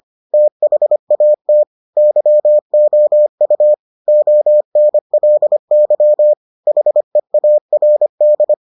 Morse code, heard as a keyed tone, about 25 wpm, 600 Hz. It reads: THAT YOU ONLY HEARD